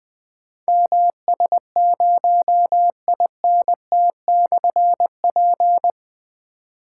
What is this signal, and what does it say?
Morse code, heard as a keyed tone, about 21 wpm, 700 Hz. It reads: MS0INT/P
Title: Difficult Call Signs 20wpm